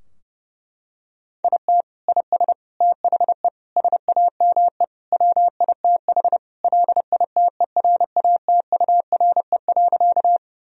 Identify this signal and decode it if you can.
Morse code, heard as a keyed tone, about 30 wpm, 700 Hz. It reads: IT IS THE SAME WITH LITERATURE.
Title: Sentences from Top 1000 Words - Review Part 3 - 30wpm